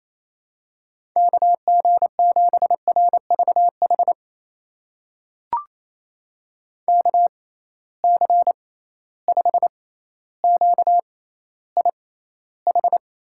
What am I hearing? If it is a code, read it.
Morse code, heard as a keyed tone, about 28 wpm, spaced as for 14 wpm, 700 Hz. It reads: KG7RVH KC5QIH
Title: USA Call Signs - 6 Characters - 6x Character Spacing - 28wpm